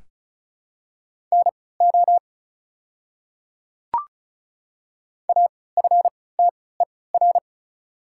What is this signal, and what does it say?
Morse code, heard as a keyed tone, about 36 wpm, spaced as for 20 wpm, 700 Hz. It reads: NO AFTER
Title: 3x Character Spacing - Top 200 Words - 35wpm